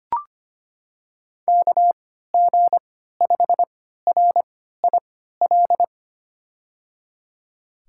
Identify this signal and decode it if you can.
Morse code, heard as a keyed tone, about 25 wpm, spaced as for 14 wpm, 700 Hz. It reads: KG5RIL
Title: USA Call Signs - 6 Characters - 3x Character Spacing - 25wpm